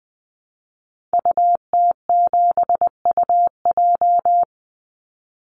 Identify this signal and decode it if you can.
Morse code, heard as a keyed tone, about 20 wpm, 700 Hz. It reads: UT7UJ
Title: Call Signs Repeated 2x - 20wpm